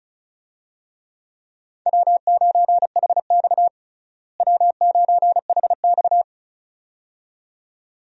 Morse code, heard as a keyed tone, about 34 wpm, 700 Hz. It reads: W9HX W9HX
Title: USA Call Signs - 4 Characters - Repeated 2x - 35wpm